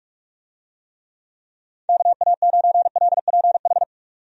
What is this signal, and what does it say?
Morse code, heard as a keyed tone, about 46 wpm, 700 Hz. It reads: KA9LPH
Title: USA Call Signs - 6 Characters - 45wpm